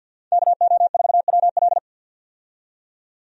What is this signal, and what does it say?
Morse code, heard as a keyed tone, about 52 wpm, 700 Hz. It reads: KO4WL
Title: USA Call Signs - 5 Characters - 50wpm